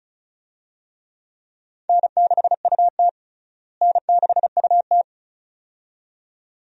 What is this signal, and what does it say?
Morse code, heard as a keyed tone, about 36 wpm, 700 Hz. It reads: N6UT N6UT
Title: USA Call Signs - 4 Characters - Repeated 2x - 35wpm